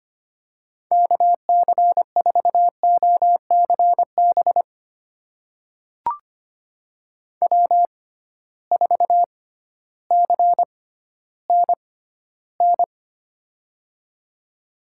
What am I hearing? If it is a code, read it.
Morse code, heard as a keyed tone, about 25 wpm, spaced as for 14 wpm, 700 Hz. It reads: KC4OCB W4CNN
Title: USA Call Signs - 6x Character Spacing - 25wpm